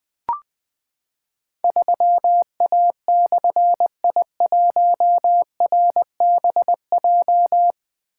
Morse code, heard as a keyed tone, about 20 wpm, 700 Hz. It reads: 3A/I1RBJ
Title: Difficult Call Signs 20wpm